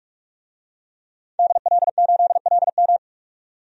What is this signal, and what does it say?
Morse code, heard as a keyed tone, about 46 wpm, 700 Hz. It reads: DL8LM
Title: DXCC-100 Call Signs - 5 Characters - 45wpm